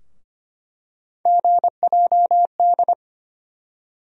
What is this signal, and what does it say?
Morse code, heard as a keyed tone, about 25 wpm, 700 Hz. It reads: GJD
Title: USA Call Sign Postfixes - 3 Letters 25wpm